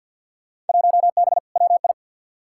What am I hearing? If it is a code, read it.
Morse code, heard as a keyed tone, about 48 wpm, 700 Hz. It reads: 1B WI
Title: Field Day Exchanges - 50wpm